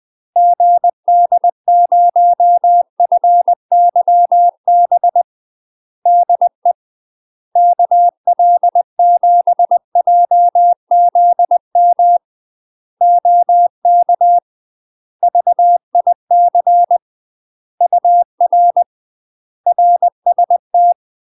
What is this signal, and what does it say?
Morse code, heard as a keyed tone, about 20 wpm, 700 Hz. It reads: GD0FYB DE KL7JZM OK VIC UR RST
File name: UK MASTER QSOs TRAINING / 20WPM / UK MASTER QSOs 0036